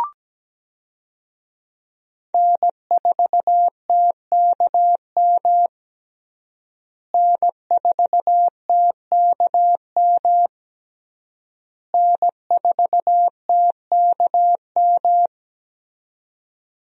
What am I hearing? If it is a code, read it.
Morse code, heard as a keyed tone, about 17 wpm, 700 Hz. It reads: N4TKM N4TKM N4TKM
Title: USA Call Signs - 5 Characters - Repeated 3x - 17wpm